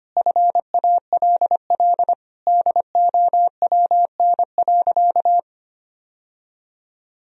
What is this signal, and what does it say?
Morse code, heard as a keyed tone, about 25 wpm, 700 Hz. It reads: FALL DOWN.
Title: Sentences from Top 1000 Words - Review Part 2 - 25wpm